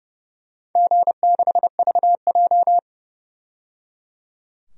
Morse code, heard as a keyed tone, about 30 wpm, 700 Hz. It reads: G6VJ
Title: DXCC-100 Call Signs - 4 Characters - 30wpm